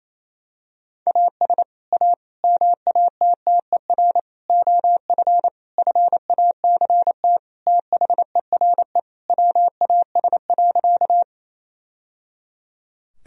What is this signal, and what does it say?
Morse code, heard as a keyed tone, about 27 wpm, 700 Hz. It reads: AS A MATTER OF FACT THERE WAS.
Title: Sentences from Top 1000 Words - Review Part 3 - 28wpm